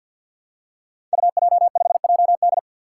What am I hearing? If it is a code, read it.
Morse code, heard as a keyed tone, about 50 wpm, 700 Hz. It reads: UJ5JD
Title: Code Groups - 5 Letter:Numbers - 50wpm